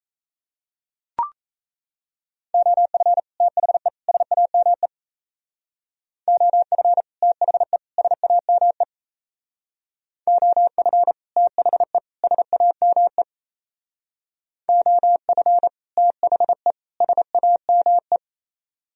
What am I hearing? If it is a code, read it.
Morse code, heard as a keyed tone, about 35 wpm, 700 Hz. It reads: OF THE SAME OF THE SAME OF THE SAME OF THE SAME